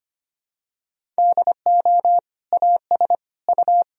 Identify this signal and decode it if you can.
Morse code, heard as a keyed tone, about 25 wpm, 700 Hz. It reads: DO AS U
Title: Sentences from Top 100 Words and Abbreviations 25wpm